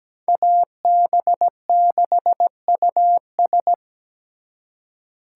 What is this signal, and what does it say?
Morse code, heard as a keyed tone, about 20 wpm, 700 Hz. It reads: AB6US
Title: Speed Racing - USA 2x2 Call Signs 17wpm